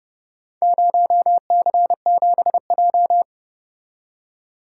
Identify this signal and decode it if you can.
Morse code, heard as a keyed tone, about 30 wpm, 700 Hz. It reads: 0C7J